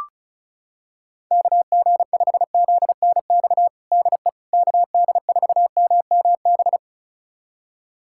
Morse code, heard as a keyed tone, about 34 wpm, 700 Hz. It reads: KG5ZNX DE KD4MMB
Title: USA Callsign DE Callsign 35wpm